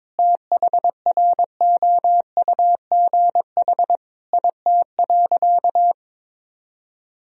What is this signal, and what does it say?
Morse code, heard as a keyed tone, about 22 wpm, 700 Hz. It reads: THROUGH IT.
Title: Sentences from Top 1000 Words - Review Part 1 - 22wpm